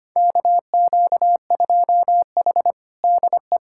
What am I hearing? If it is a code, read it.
Morse code, heard as a keyed tone, about 25 wpm, 700 Hz. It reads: KQ2H DE
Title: USA Callsign DE Callsign - 4 Characters - 25wpm